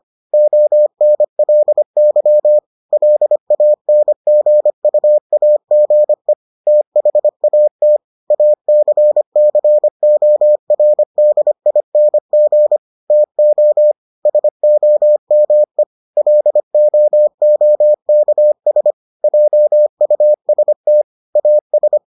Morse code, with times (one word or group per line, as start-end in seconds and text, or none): 0.3-2.6 ONLY
2.9-6.3 LANGUAGE
6.7-8.0 THAT
8.3-12.8 ACCORDING
13.1-13.9 TO
14.3-15.8 SOME
16.2-18.9 LOOKS
19.2-21.0 JUST
21.4-22.0 AS